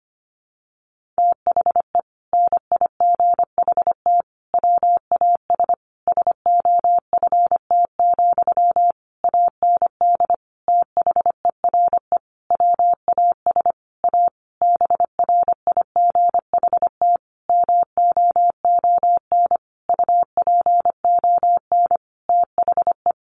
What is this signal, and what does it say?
Morse code, heard as a keyed tone, about 25 wpm, 700 Hz. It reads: THE NIGHT WAS SOFT, AND THERE WAS A BRIGHT MOON UPON THE
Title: Sentences from Top 1500 Words 25wpm